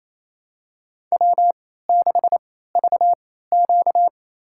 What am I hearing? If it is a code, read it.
Morse code, heard as a keyed tone, about 28 wpm, spaced as for 16 wpm, 700 Hz. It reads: W6VQ
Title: USA Call Signs - 4 Characters - 3x Character Spacing - 28wpm